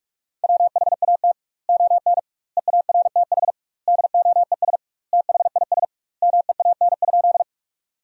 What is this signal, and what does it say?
Morse code, heard as a keyed tone, about 46 wpm, 700 Hz. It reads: WHAT ON EARTH DOES THIS MEAN?
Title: 2x Word-Spacing - Sentences from Top 200 Words - 45wpm